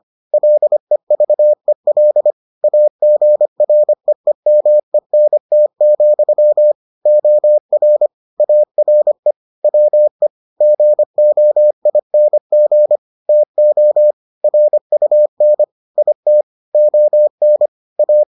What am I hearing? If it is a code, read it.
Morse code, heard as a keyed tone, about 25 wpm, 600 Hz. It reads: LEVEL AGREEMENT, OR ARE WE GOING TO RUN IT ON A